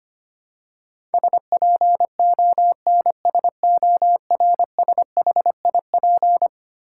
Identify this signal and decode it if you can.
Morse code, heard as a keyed tone, about 25 wpm, 700 Hz. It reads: SPONSORSHIP
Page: morse-code-ninja-practice-25wpm